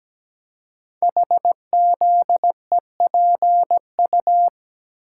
Morse code, heard as a keyed tone, about 17 wpm, 700 Hz. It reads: HZEPU
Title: Code Groups - 5 Letters - 17wpm